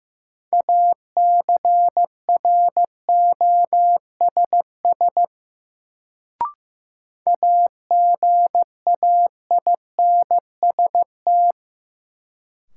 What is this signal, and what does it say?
Morse code, heard as a keyed tone, about 15 wpm, 700 Hz. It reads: ACROSS AGAINST
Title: Sets of 2 Words - New Word Drill - 15wpm